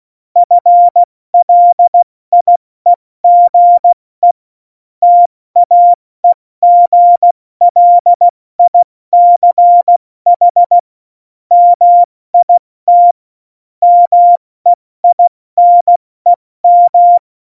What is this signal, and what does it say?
Morse code, heard as a keyed tone, about 16 wpm, 700 Hz. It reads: FLIEGE TAEGLICH MIT MEINEM
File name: Morsekurs / Witze / Witz 19. Januar 2021, 70 F80